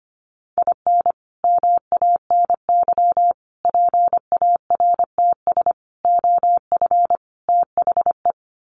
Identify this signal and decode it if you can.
Morse code, heard as a keyed tone, about 25 wpm, 700 Hz. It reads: IN MANY PARTS OF THE
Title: Sets of 5 Words Encore 25wpm